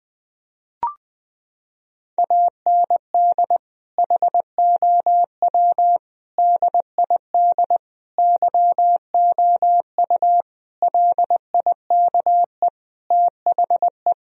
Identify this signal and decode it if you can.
Morse code, heard as a keyed tone, about 20 wpm, 700 Hz. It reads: AND HOW DID YOU LIKE THE
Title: Sentences from Top 1000 Words - Review Part 4 - 20wpm